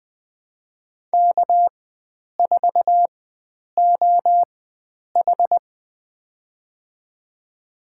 Morse code, heard as a keyed tone, about 20 wpm, spaced as for 9 wpm, 700 Hz. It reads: K4OH
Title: USA Call Signs - 4 Characters - 4x Character Spacing - 20wpm